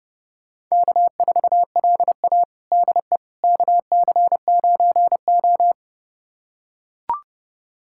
Morse code, heard as a keyed tone, about 31 wpm, 700 Hz. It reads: K4LA DE KC9O
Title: USA Callsign DE Callsign - 4 Characters - 30wpm